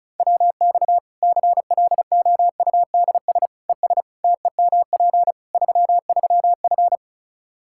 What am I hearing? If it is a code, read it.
Morse code, heard as a keyed tone, about 36 wpm, 700 Hz. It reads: WX CLOUDS ES TEMP 33F
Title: K1USN SST QSO Practice 35wpm